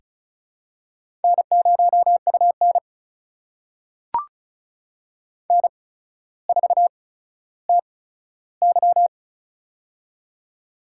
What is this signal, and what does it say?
Morse code, heard as a keyed tone, about 36 wpm, spaced as for 15 wpm, 700 Hz. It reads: N0UN N4TY
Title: USA Call Signs - 4 Characters - 8x Character Spacing - 35wpm